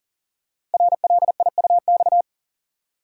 Morse code, heard as a keyed tone, about 42 wpm, 700 Hz. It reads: RLIUX
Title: Code Groups - 5 Letter:Numbers - 40wpm